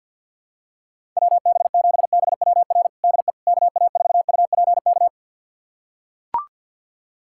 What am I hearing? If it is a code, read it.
Morse code, heard as a keyed tone, about 45 wpm, 700 Hz. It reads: WB7BWR DE KA4UPK
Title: USA Callsign DE Callsign 50wpm